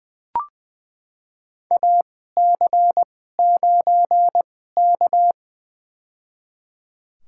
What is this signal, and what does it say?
Morse code, heard as a keyed tone, about 20 wpm, spaced as for 14 wpm, 700 Hz. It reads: AC9K
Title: USA Call Signs - 4 Characters - 2x Character Spacing - 20wpm